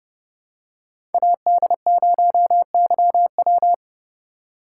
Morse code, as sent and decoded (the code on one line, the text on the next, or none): .- -.. ----- -.-- .--
AD0YW